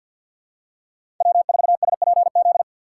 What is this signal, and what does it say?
Morse code, heard as a keyed tone, about 53 wpm, 700 Hz. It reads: W4SPZ
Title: USA Call Signs - 6x Character Spacing - 50wpm